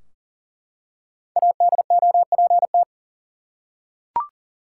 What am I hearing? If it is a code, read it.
Morse code, heard as a keyed tone, about 40 wpm, 700 Hz. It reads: ADOPT
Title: Five Letter Words - 40wpm